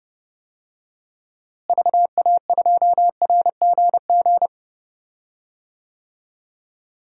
Morse code, heard as a keyed tone, about 30 wpm, 700 Hz. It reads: VA2RGG
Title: DXCC-100 Call Signs - 6 Characters - Repeated 2x - 30wpm